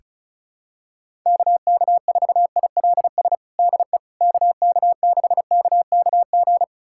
Morse code, heard as a keyed tone, about 34 wpm, 700 Hz. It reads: KK4ILS DE KK6KKG
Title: USA Callsign DE Callsign - 6 Characters - 35wpm